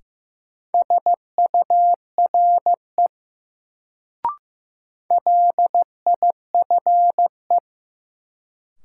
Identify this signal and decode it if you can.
Morse code, heard as a keyed tone, about 15 wpm, 700 Hz. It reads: SURE LIFE
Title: Sentences from Top 300 Words - New Word Drill - 15wpm